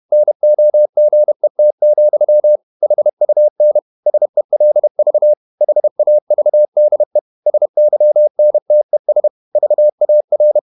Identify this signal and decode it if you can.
Morse code, heard as a keyed tone, about 31 wpm, 600 Hz. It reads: NOGET, HUN SELV HAVDE SYNTES VAR
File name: Ferie QRQ 31wpm